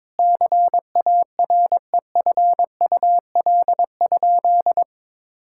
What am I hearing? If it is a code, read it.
Morse code, heard as a keyed tone, about 21 wpm, 700 Hz. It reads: CAREFUL?
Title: Sentences from Top 1600 Words 22wpm